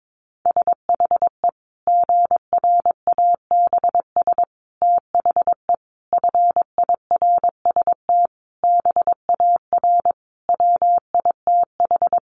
Morse code, heard as a keyed tone, about 22 wpm, 700 Hz. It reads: SHE GRABS THE FIRST BAR WITH
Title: Easy Phrases 22wpm